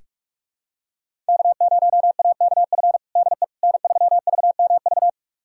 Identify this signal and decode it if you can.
Morse code, heard as a keyed tone, about 45 wpm, 700 Hz. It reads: K0AKF DE N3VMV
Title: USA Callsign DE Callsign - 5 Characters - 45wpm